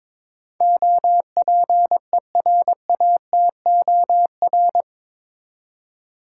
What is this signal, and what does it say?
Morse code, heard as a keyed tone, about 22 wpm, 700 Hz. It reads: OPERATOR
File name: practice / 22wpm / Eight Letter Words - 22wpm